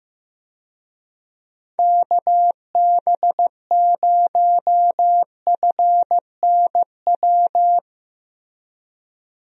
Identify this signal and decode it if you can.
Morse code, heard as a keyed tone, about 15 wpm, 700 Hz. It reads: KB0FNW
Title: USA Call Signs - 6 Characters - Repeated 2x - 15wpm